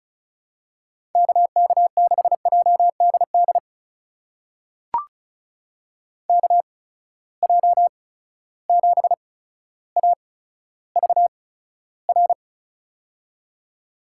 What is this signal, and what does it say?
Morse code, heard as a keyed tone, about 36 wpm, spaced as for 15 wpm, 700 Hz. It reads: KK6JDD KJ7AVR
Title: USA Call Signs - 6 Characters - 8x Character Spacing - 35wpm